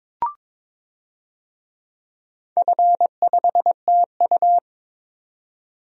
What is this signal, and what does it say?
Morse code, heard as a keyed tone, about 22 wpm, 690 Hz. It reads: F5TU
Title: DXCC-100 Call Signs - 4 Characters - 22wpm